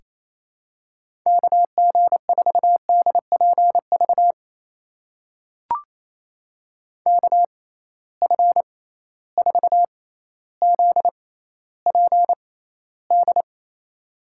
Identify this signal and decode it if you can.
Morse code, heard as a keyed tone, about 27 wpm, spaced as for 14 wpm, 700 Hz. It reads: KG4DPV KF4ZPD
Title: USA Call Signs - 6 Characters - 6x Character Spacing - 28wpm